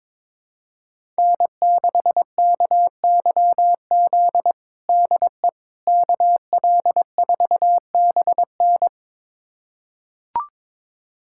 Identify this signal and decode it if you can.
Morse code, heard as a keyed tone, about 22 wpm, 700 Hz. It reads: N6KYZ DE KL4BN
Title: USA Callsign DE Callsign - 5 Characters - 22wpm